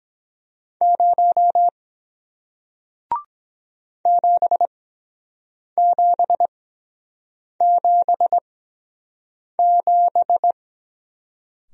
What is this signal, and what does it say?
Morse code, heard as a keyed tone, about 23 wpm, spaced as for 7 wpm, 700 Hz. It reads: 0 7777